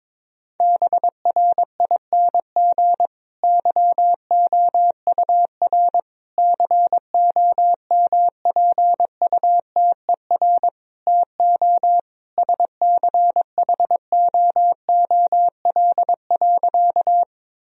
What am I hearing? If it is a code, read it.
Morse code, heard as a keyed tone, about 23 wpm, 700 Hz. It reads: BRING YOUR COMPUTER TO SCHOOL.
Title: Sentences from Top 1000 Words - Review Part 3 - 22wpm